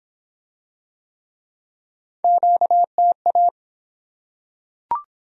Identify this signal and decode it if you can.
Morse code, heard as a keyed tone, about 24 wpm, 700 Hz. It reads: QTA